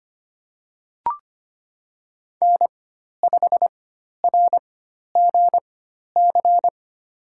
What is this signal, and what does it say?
Morse code, heard as a keyed tone, about 25 wpm, spaced as for 12 wpm, 700 Hz. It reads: N5RGC
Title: USA Call Signs - 5 Characters - 4x Character Spacing - 25wpm